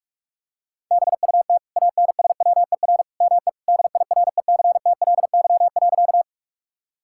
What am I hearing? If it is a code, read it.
Morse code, heard as a keyed tone, about 46 wpm, 700 Hz. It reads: BUT ANSWER ME DIRECTLY.
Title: Sentences from Top 1000 Words - Review Part 5 - 45wpm